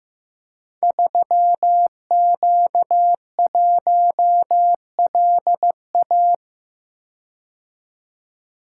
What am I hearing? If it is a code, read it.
Morse code, heard as a keyed tone, about 15 wpm, 700 Hz. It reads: 3Q1LA
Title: Code Groups - 5 Letter:Numbers - 15wpm